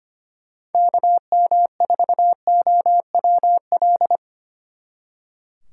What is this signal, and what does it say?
Morse code, heard as a keyed tone, about 30 wpm, 700 Hz. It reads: KM4OWL
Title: Speed Racing - USA Call Signs 25wpm